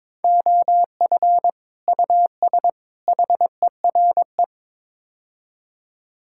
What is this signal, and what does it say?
Morse code, heard as a keyed tone, about 22 wpm, 700 Hz. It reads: OF US HERE
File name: practice / 22wpm / Build A Phrase - 5-Word Phrases - 22wpm